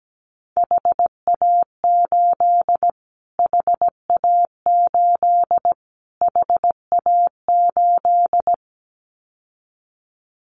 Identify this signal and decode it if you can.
Morse code, heard as a keyed tone, about 17 wpm, 700 Hz. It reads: HA8 HA8 HA8
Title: Callsign Prefixes for Top 25 DXCC - Repeated 3x - 17wpm